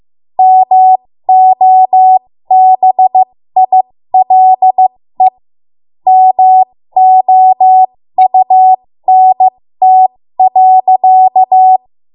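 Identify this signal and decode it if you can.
Morse code, heard as a keyed tone, about 15 wpm, spaced as for 13 wpm, 750 Hz. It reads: MOBILE MOUNT.
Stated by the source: ARRL Morse code practice at 13 WPM.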